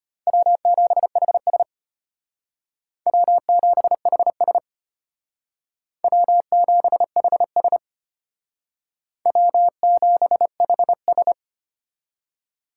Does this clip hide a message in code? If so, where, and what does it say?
Yes: Morse code from 0.3 to 11.3 s, heard as a keyed tone, about 32 wpm, 700 Hz. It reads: W7HS W7HS W7HS W7HS